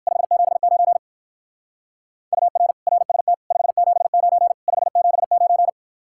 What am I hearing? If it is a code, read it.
Morse code, heard as a keyed tone, about 59 wpm, 700 Hz. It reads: 579 UR RST 579 579